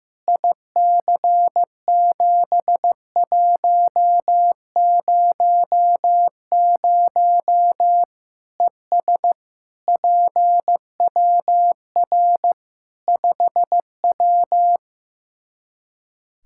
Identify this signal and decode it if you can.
Morse code, heard as a keyed tone, about 15 wpm, 700 Hz. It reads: IC7100 ES PWR 5W
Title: Practice QSO - Exchange 2 - Info - 15wpm